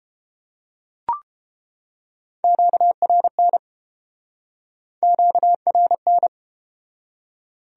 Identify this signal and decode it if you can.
Morse code, heard as a keyed tone, about 30 wpm, 700 Hz. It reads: QRN QRN